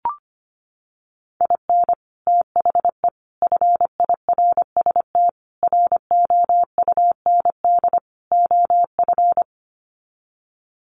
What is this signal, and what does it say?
Morse code, heard as a keyed tone, about 25 wpm, 700 Hz. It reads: IN THE FIRST ROUND OF
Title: Sets of 5 Words Encore 25wpm